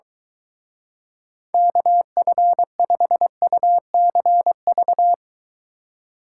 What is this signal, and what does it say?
Morse code, heard as a keyed tone, about 23 wpm, 700 Hz. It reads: KF5UCV